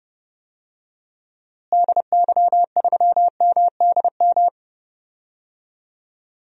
Morse code, heard as a keyed tone, about 30 wpm, 700 Hz. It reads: DY3MDM
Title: DXCC-100 Call Signs - 6 Characters - Repeated 2x - 30wpm